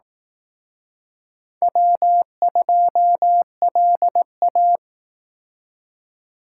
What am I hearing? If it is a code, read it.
Morse code, heard as a keyed tone, about 18 wpm, 700 Hz. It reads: W2LA
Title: Speed Racing - USA 1x2 Call Signs 15wpm